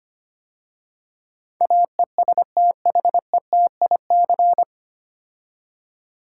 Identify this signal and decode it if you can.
Morse code, heard as a keyed tone, about 25 wpm, 700 Hz. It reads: AESTHETIC
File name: practice / 25wpm / 9 Letter Words - First 3 Letters Given - 25wpm